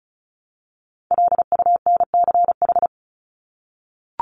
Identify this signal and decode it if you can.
Morse code, heard as a keyed tone, about 34 wpm, 700 Hz. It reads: LUNCH
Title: Top 1751-2000 Words 35wpm